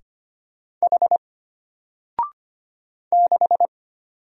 Morse code, heard as a keyed tone, about 25 wpm, 700 Hz. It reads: H 6
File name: Single-Letter-Number-25wpm